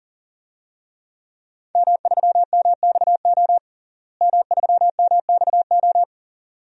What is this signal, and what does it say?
Morse code, heard as a keyed tone, about 37 wpm, 700 Hz. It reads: M3MXO M3MXO
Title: DXCC-100 Call Signs - 5 Characters - Repeated 2x - 40wpm